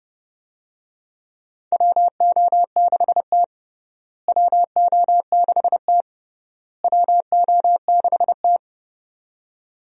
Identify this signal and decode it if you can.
Morse code, heard as a keyed tone, about 30 wpm, 700 Hz. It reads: WO6T WO6T WO6T
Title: USA Call Signs - 4 Characters - Repeated 3x - 30wpm